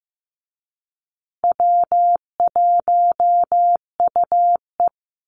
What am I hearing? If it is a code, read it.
Morse code, heard as a keyed tone, about 15 wpm, 700 Hz. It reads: W1UE
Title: Easy Call Signs Repeated 3x - 15wpm